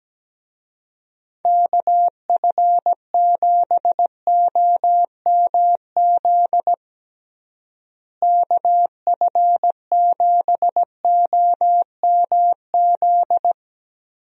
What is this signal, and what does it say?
Morse code, heard as a keyed tone, about 17 wpm, 700 Hz. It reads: KF7OMZ KF7OMZ
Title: USA Call Signs - 6 Characters - Repeated 2x - 17wpm